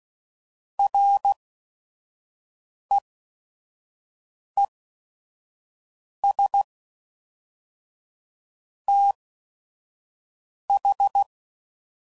Morse code, heard as a keyed tone, about 16 wpm, spaced as for 5 wpm, 800 Hz. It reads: REES TH
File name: CW Code Groups @5WPM / QSOs / QSO WD0FBT